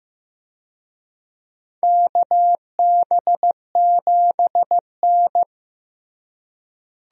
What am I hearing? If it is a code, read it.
Morse code, heard as a keyed tone, about 15 wpm, 700 Hz. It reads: KB7N
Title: USA Call Signs - 4 Characters - Repeated 3x - 15wpm